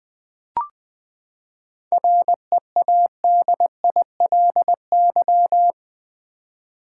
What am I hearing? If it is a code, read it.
Morse code, heard as a keyed tone, about 20 wpm, 700 Hz. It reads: READILY
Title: Seven Letter Words - 20wpm